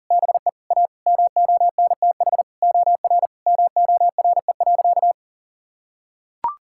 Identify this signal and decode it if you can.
Morse code, heard as a keyed tone, about 40 wpm, 700 Hz. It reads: BE A MONTH OR MORE.
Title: Sentences from Top 1000 Words - Review Part 5 - 40wpm